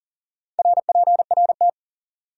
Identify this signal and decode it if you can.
Morse code, heard as a keyed tone, about 52 wpm, 700 Hz. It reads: RPRT